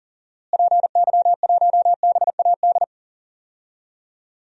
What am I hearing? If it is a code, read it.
Morse code, heard as a keyed tone, about 40 wpm, 700 Hz. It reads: PY1BAD
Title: DXCC-100 Call Signs - 6 Characters - Repeated 3x - 40wpm